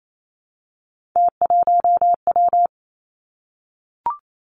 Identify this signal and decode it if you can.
Morse code, heard as a keyed tone, about 28 wpm, 700 Hz. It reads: T1W
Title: Three Challenging Letter-Number Combinations 28wpm